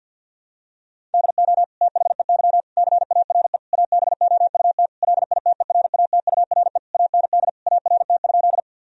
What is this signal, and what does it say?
Morse code, heard as a keyed tone, about 52 wpm, 700 Hz. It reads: DO THEY CARE ABOUT LITERATURE AND ART?
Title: Sentences from Top 1000 Words - Review Part 3 - 50wpm